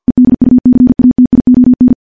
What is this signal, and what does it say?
Morse code, heard as a keyed tone, about 50 wpm, 250 Hz. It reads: EDUCATION
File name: morse / morse-training-50WPM